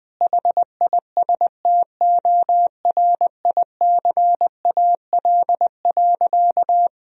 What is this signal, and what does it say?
Morse code, heard as a keyed tone, about 20 wpm, 700 Hz. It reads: HISTORICAL.
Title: Sentences from Top 1000 Words - Review Part 4 - 20wpm